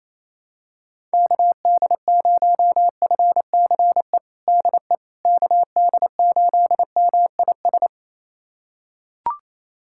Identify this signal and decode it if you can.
Morse code, heard as a keyed tone, about 28 wpm, 700 Hz. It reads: KD0FCE DE KD8MIS
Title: USA Callsign DE Callsign 28wpm